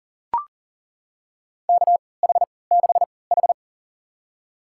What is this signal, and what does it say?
Morse code, heard as a keyed tone, about 40 wpm, spaced as for 23 wpm, 700 Hz. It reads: KH6H
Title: USA Call Signs - 4 Characters - 3x Character Spacing - 40wpm